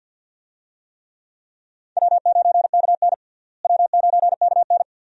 Morse code, heard as a keyed tone, about 52 wpm, 700 Hz. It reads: W9KN W9KN
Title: USA Call Signs - 4 Characters - Repeated 2x - 50wpm